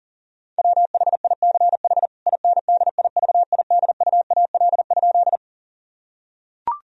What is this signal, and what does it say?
Morse code, heard as a keyed tone, about 40 wpm, 690 Hz. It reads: WHICH INDIVIDUAL?
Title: Sentences from Top 1000 Words - Review Part 4 - 40wpm